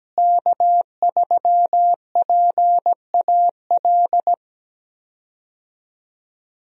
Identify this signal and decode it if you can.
Morse code, heard as a keyed tone, about 17 wpm, 700 Hz. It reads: K3PAL
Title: USA Call Signs - 5 Characters - Repeated 2x - 17wpm